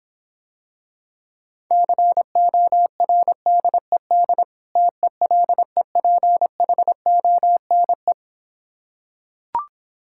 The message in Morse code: CORDED TELEPHONE

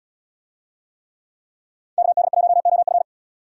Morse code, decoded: KD1OF

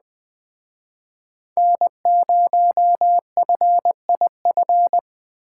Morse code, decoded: N0FIF

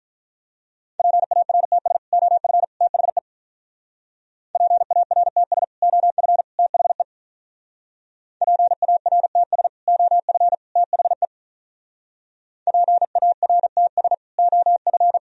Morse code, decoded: PARTS OF THE PARTS OF THE PARTS OF THE PARTS OF

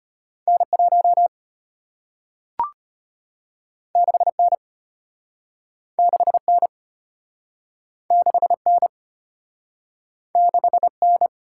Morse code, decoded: N1 6N 6N 6N 6N

